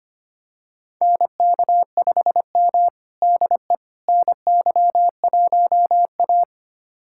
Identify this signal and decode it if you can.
Morse code, heard as a keyed tone, about 25 wpm, 700 Hz. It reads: NK5M DE NY1A
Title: USA Callsign DE Callsign - 4 Characters - 25wpm